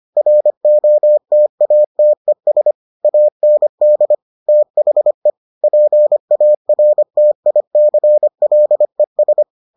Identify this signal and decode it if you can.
Morse code, heard as a keyed tone, about 25 wpm, 600 Hz. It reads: ROTATES AND THE PARTICLES